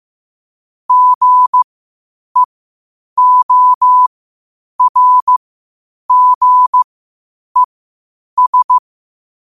Morse code, heard as a keyed tone, about 15 wpm, spaced as for 8 wpm, 1000 Hz. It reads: GEORGES
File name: qotdfiles / 8wpm